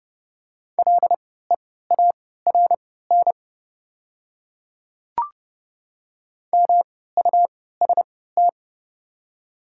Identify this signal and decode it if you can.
Morse code, heard as a keyed tone, about 30 wpm, spaced as for 17 wpm, 690 Hz. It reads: LEARN MUST
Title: 3x Character Spacing - Top 300 Words - 30wpm